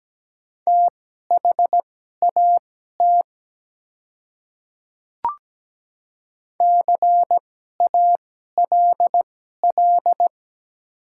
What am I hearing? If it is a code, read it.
Morse code, heard as a keyed tone, about 17 wpm, spaced as for 12 wpm, 700 Hz. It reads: THAT CALL
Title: 2x Character Spacing - Top 100 Words - 17wpm